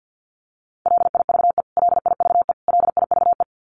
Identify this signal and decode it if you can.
Morse code, heard as a keyed tone, about 50 wpm, 700 Hz. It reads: LIVE LIVE LIVE
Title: Top 300 Words Repeated 50wpm